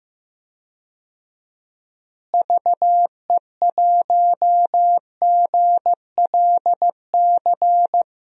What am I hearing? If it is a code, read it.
Morse code, heard as a keyed tone, about 16 wpm, 700 Hz. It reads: VE1GLC
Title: DXCC-100 Call Signs - 6 Characters - Repeated 2x - 15wpm